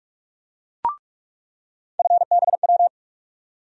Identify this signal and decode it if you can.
Morse code, heard as a keyed tone, about 42 wpm, 700 Hz. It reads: FBW
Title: USA Call Sign Postfixes - 3 Letters 45wpm